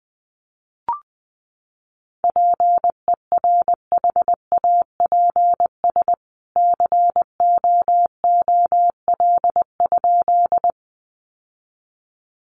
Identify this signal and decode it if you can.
Morse code, heard as a keyed tone, about 20 wpm, 700 Hz. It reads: PERHAPS COOL?
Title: Sentences from Top 1200 Words 20wpm